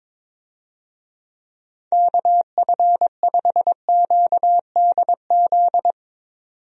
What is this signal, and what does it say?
Morse code, heard as a keyed tone, about 22 wpm, 700 Hz. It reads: KF5QDZ
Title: USA Call Signs - 6 Characters - 22wpm